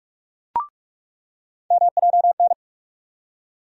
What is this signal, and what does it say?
Morse code, heard as a keyed tone, about 43 wpm, 700 Hz. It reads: MJN
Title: USA Call Sign Postfixes - 3 Letters 45wpm